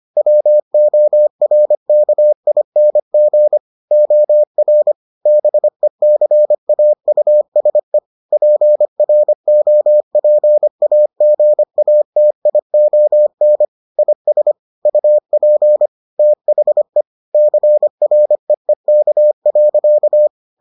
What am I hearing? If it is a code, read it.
Morse code, heard as a keyed tone, about 25 wpm, 600 Hz. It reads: WORKING OR BECAUSE PROPAGATION IS UP THE CREEK.